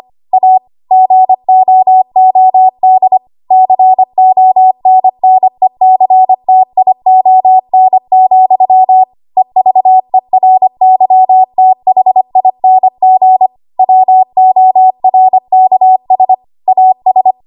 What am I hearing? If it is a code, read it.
Morse code, heard as a keyed tone, about 25 wpm, 750 Hz. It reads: A GOOD CONNECTION, EVERYTHING WORKS AS
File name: Archive / 25 WPM / 25WPM